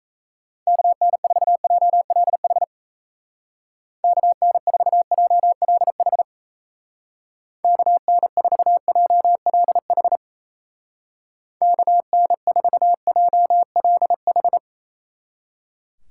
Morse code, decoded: KN4JLH KN4JLH KN4JLH KN4JLH

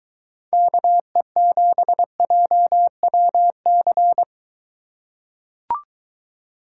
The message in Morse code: KE7JWC

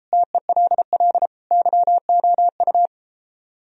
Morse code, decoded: TELL YOU

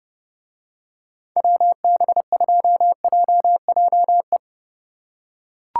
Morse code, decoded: WB2JJE

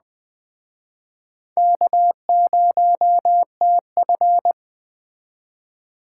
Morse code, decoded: K0TF

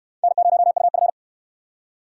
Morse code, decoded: N0UF

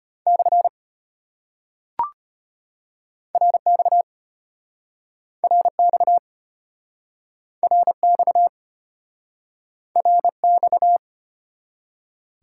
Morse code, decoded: / RX RX RX RX

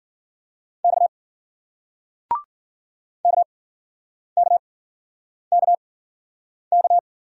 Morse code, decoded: X KKKK

